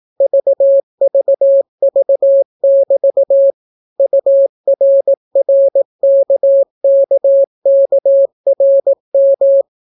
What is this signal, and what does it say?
Morse code, heard as a keyed tone, about 18 wpm, 550 Hz. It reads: VVV<BT> URRKKKRM